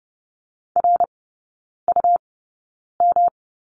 Morse code, heard as a keyed tone, about 30 wpm, spaced as for 9 wpm, 700 Hz. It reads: RUM
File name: Lesson 18e - Letter m - Three Characters 30wpm